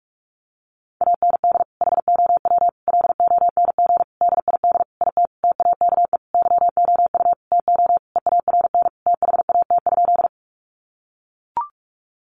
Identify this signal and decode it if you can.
Morse code, heard as a keyed tone, about 44 wpm, 700 Hz. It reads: AND HOW LONG DID IT TAKE YOU TO EARN THAT?
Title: Sentences from Top 1500 Words 45wpm